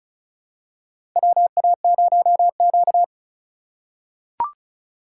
Morse code, heard as a keyed tone, about 34 wpm, 700 Hz. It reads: WA0Q
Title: USA Call Signs - 4 Characters - 2x Character Spacing - 35wpm